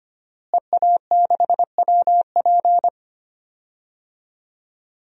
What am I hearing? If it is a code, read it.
Morse code, heard as a keyed tone, about 25 wpm, 700 Hz. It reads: EA6WP
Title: DXCC-100 Call Signs - 5 Characters - Repeated 3x - 25wpm